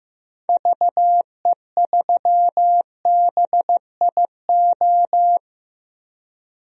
Morse code, heard as a keyed tone, about 15 wpm, 700 Hz. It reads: VE3BIO
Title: DXCC-100 Call Signs - 6 Characters - 15wpm